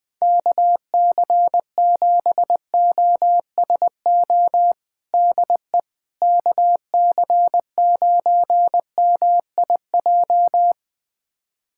Morse code, heard as a keyed tone, about 20 wpm, 700 Hz. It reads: KC7OSO DE KC9MIJ
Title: USA Callsign DE Callsign - 6 Characters - 20wpm